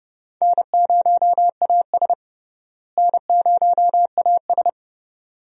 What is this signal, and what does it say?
Morse code, heard as a keyed tone, about 31 wpm, 700 Hz. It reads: N0AS N0AS
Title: USA Call Signs - 4 Characters - Repeated 2x - 30wpm